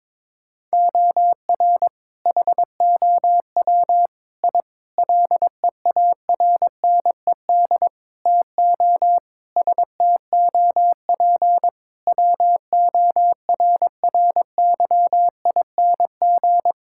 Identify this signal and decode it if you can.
Morse code, heard as a keyed tone, about 22 wpm, 700 Hz. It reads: OR HOW I LEARNED TO STOP WORRYING
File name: practice / 22wpm / Sentences with 8-Word Phrases - 22wpm